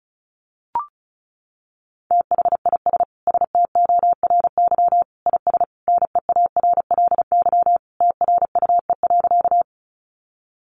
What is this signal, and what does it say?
Morse code, heard as a keyed tone, about 34 wpm, 700 Hz. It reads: THIS STORY IS NEARLY TRUE.
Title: Sentences from Top 1300 Words 35wpm